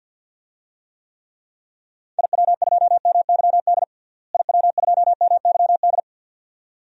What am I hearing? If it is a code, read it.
Morse code, heard as a keyed tone, about 51 wpm, 700 Hz. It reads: IW2MYD IW2MYD
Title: DXCC-100 Call Signs - 6 Characters - Repeated 2x - 50wpm